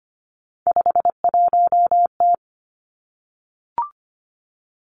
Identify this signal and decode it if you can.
Morse code, heard as a keyed tone, about 25 wpm, 700 Hz. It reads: 51T
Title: Three Challenging Letter-Number Combinations 25wpm